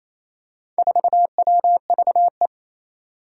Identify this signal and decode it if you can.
Morse code, heard as a keyed tone, about 28 wpm, 700 Hz. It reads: 4WVE
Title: Four Challenging Letter-Number Combinations 28wpm